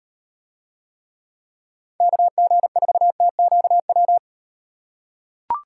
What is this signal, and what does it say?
Morse code, heard as a keyed tone, about 37 wpm, 700 Hz. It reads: KG4TQW